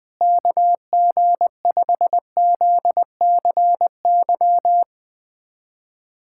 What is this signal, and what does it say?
Morse code, heard as a keyed tone, about 21 wpm, 700 Hz. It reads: KG5ZCY
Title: USA Callsign DE Callsign - 6 Characters - 20wpm